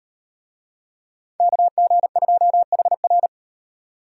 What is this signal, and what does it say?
Morse code, heard as a keyed tone, about 38 wpm, 700 Hz. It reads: KG2HR